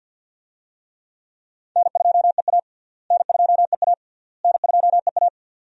Morse code, heard as a keyed tone, about 48 wpm, 700 Hz. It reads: N2EA N2EA N2EA
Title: USA Call Signs - 4 Characters - Repeated 3x - 50wpm